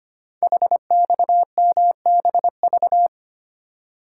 Morse code, heard as a keyed tone, about 25 wpm, 700 Hz. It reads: HXMBV
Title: Code Groups - 5 Letters - 25wpm